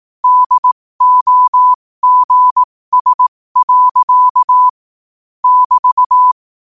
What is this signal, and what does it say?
Morse code, heard as a keyed tone, about 18 wpm, spaced as for 15 wpm, 1000 Hz. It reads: DOGS. <BT>
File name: qotdfiles / 15wpm